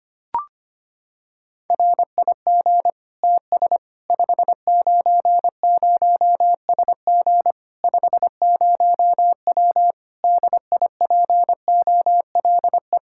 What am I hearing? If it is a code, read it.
Morse code, heard as a keyed tone, about 25 wpm, 700 Hz. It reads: RIG TS 590SG 50W DIPOLE
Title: Practice QSO - Exchange 2 - Info - 25wpm